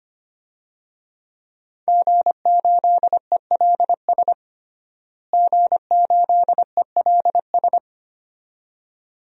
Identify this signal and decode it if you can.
Morse code, heard as a keyed tone, about 25 wpm, 700 Hz. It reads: G8ELS G8ELS
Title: DXCC-100 Call Signs - 5 Characters - Repeated 2x - 25wpm